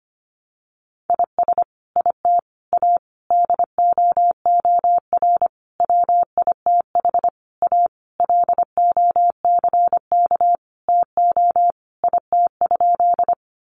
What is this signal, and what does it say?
Morse code, heard as a keyed tone, about 25 wpm, 700 Hz. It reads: IS IT A DOOR WITH A LOCK TO IT?
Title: Sentences from Top 1300 Words 25wpm